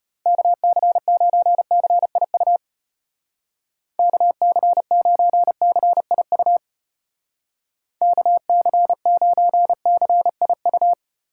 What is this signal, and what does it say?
Morse code, heard as a keyed tone, about 32 wpm, 700 Hz. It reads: KC9CIU KC9CIU KC9CIU